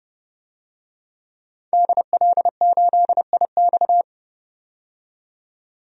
Morse code, heard as a keyed tone, about 32 wpm, 700 Hz. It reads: DL8IX
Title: DXCC-100 Call Signs - 5 Characters - 30wpm